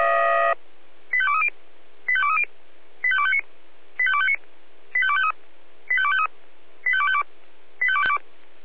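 Five-tone selective call (ZVEI), heard as quick runs of five tones, tones of about 70 ms. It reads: 85230 85230 85259 85259 85253 85253 85252
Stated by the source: ZVEI five-tone call received on an SDR.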